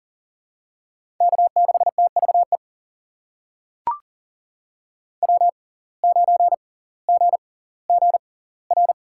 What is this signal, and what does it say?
Morse code, heard as a keyed tone, about 39 wpm, spaced as for 20 wpm, 700 Hz. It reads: K6TVE W9GGR
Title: USA Call Signs - 5 Characters - 6x Character Spacing - 40wpm